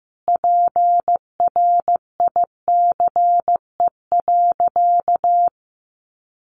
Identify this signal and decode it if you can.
Morse code, heard as a keyed tone, about 15 wpm, 700 Hz. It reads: PRICE.
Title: Sentences from Top 600 Words 15wpm